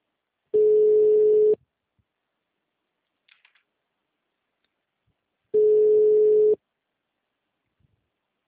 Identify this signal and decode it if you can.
Ringback tone (Europe, 425 Hz).